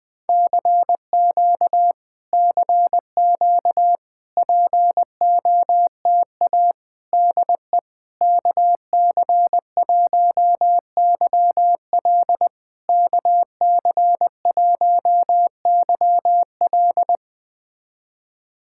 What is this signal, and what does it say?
Morse code, heard as a keyed tone, about 20 wpm, 700 Hz. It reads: CQ CQ POTA DE KC1YL KC1YL
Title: POTA QSO Practice - 20wpm